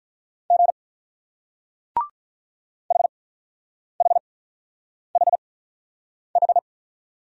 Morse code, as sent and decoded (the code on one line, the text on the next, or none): --. / .... .... .... ....
G HHHH